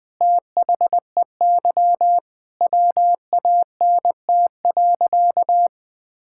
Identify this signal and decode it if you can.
Morse code, heard as a keyed tone, about 20 wpm, 700 Hz. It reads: THEY WANT.
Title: Sentences with 5-Word Phrases - Spoken Beginning - 20wpm